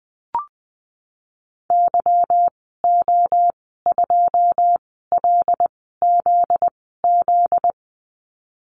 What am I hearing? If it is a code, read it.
Morse code, heard as a keyed tone, about 20 wpm, spaced as for 15 wpm, 700 Hz. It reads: YO2LZZ
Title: 245 Lesson 39g - Letter z - Call Signs 20wpm